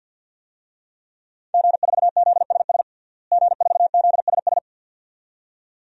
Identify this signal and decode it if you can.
Morse code, heard as a keyed tone, about 52 wpm, 700 Hz. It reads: G4ZSS G4ZSS
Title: DXCC-100 Call Signs - 5 Characters - Repeated 2x - 50wpm